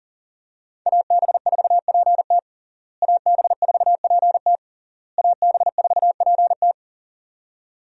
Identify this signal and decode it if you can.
Morse code, heard as a keyed tone, about 40 wpm, 700 Hz. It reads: AB4PT AB4PT AB4PT
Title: USA Call Signs - 5 Characters - Repeated 3x - 40wpm